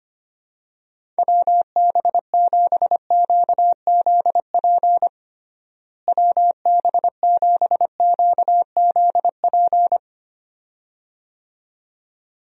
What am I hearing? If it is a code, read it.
Morse code, heard as a keyed tone, about 25 wpm, 700 Hz. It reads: WB7QZP WB7QZP
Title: USA Call Signs - 6 Characters - Repeated 2x - 25wpm